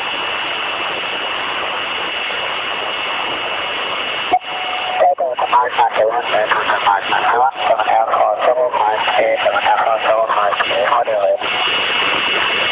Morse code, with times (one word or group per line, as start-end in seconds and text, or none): none